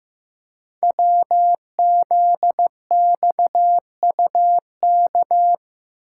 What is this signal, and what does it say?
Morse code, heard as a keyed tone, about 15 wpm, 700 Hz. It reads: WZXUK
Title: Code Groups - 5 Letter:Numbers - 15wpm